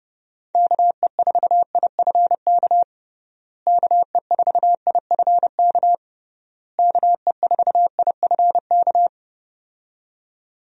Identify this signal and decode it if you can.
Morse code, heard as a keyed tone, about 31 wpm, 700 Hz. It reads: KE4IFK KE4IFK KE4IFK
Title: USA Call Signs - 6 Characters - Repeated 3x - 30wpm